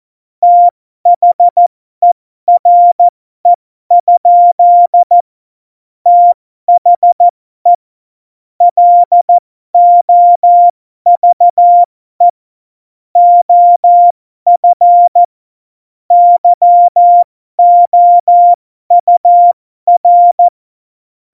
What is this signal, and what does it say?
Morse code, heard as a keyed tone, about 14 wpm, spaced as for 12 wpm, 700 Hz. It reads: THERE? THE LOVE OF YOUR
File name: Morsekurs / Witze